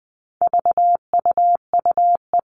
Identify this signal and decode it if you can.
Morse code, heard as a keyed tone, about 20 wpm, 700 Hz. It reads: VUUE
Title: All Cut Numbers - 4-Digit 20wpm